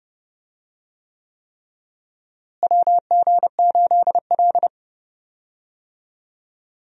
Morse code, heard as a keyed tone, about 30 wpm, 700 Hz. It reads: WG8L
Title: USA Call Signs - 4 Characters - Repeated 3x - 30wpm